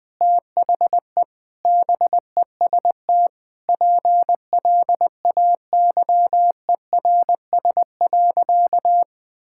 Morse code, heard as a keyed tone, about 21 wpm, 700 Hz. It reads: THE BEST PLAYERS.
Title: Sentences with 5-Word Phrases - 20wpm